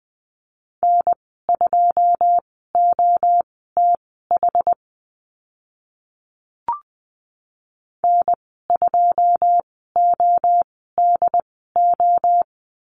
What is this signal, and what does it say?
Morse code, heard as a keyed tone, about 20 wpm, spaced as for 14 wpm, 700 Hz. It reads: N2OTH N2ODO
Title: 082 Lesson 14g - Number 2 - Call Signs 20wpm